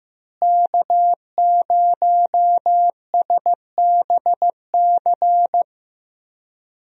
Morse code, heard as a keyed tone, about 15 wpm, 700 Hz. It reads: K0SBC
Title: USA Call Signs - 5 Characters - Repeated 3x - 15wpm